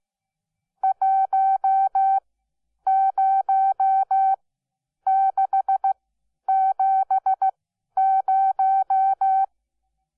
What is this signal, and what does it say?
Morse code, heard as a keyed tone, about 15 wpm, spaced as for 9 wpm, 770 Hz. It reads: 10670
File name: Onlinelehrgang / morsen / DJ4UF Morsen12